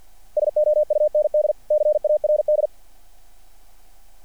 Morse code, heard as a keyed tone, about 49 wpm, 600 Hz. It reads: SOUND CARD
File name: archived / CRSnet